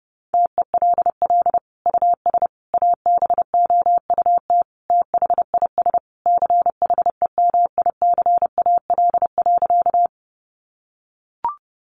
Morse code, heard as a keyed tone, about 30 wpm, 700 Hz. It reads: TELL US ABOUT THIS CHEMICAL.
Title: Sentences from Top 1200 Words 30wpm